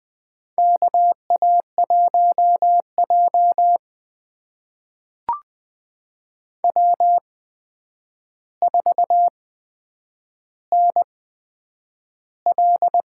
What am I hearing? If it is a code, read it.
Morse code, heard as a keyed tone, about 20 wpm, spaced as for 9 wpm, 700 Hz. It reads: KA1J W4NL
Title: USA Call Signs - 4 Characters - 8x Character Spacing - 20wpm